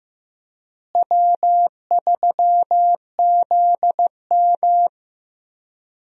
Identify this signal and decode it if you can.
Morse code, heard as a keyed tone, about 15 wpm, 700 Hz. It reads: W3ZM
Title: USA Call Signs - 4 Characters - Repeated 3x - 15wpm